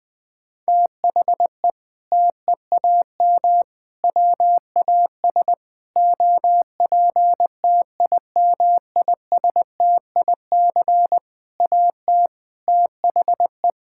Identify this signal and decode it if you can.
Morse code, heard as a keyed tone, about 21 wpm, 700 Hz. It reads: THE TEAM WAS OPTIMISTIC AT THE
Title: Sentences with 6-Word Phrases - 20wpm